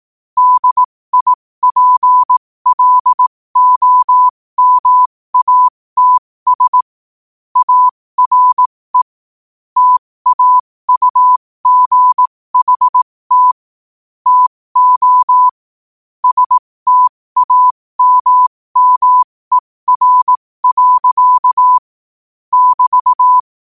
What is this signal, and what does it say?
Morse code, heard as a keyed tone, about 18 wpm, spaced as for 15 wpm, 1000 Hz. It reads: DIPLOMATS ARE TAUGHT TO STAMMER. <BT>
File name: qotdfiles / 15wpm